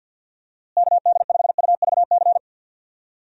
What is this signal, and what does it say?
Morse code, heard as a keyed tone, about 52 wpm, 700 Hz. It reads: KD5UVC
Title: USA Call Signs - 6 Characters - Repeated 3x - 50wpm